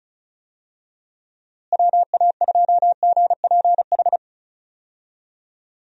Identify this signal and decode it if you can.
Morse code, heard as a keyed tone, about 34 wpm, 700 Hz. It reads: WA2GPH
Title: USA Call Signs - 6 Characters - Repeated 3x - 35wpm